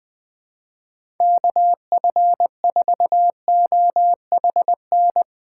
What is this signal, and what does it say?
Morse code, heard as a keyed tone, about 21 wpm, 700 Hz. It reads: KF4OHN
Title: USA Callsign DE Callsign - 6 Characters - 20wpm